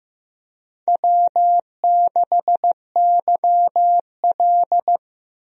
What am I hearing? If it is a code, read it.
Morse code, heard as a keyed tone, about 15 wpm, 700 Hz. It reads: W6YL
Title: USA Call Signs - 4 Characters - 15wpm